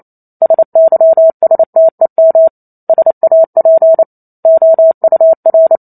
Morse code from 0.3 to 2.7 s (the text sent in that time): SYSTEM